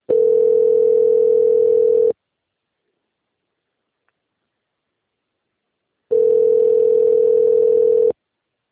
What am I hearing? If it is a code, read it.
Ringback tone (North America), 440 + 480 Hz.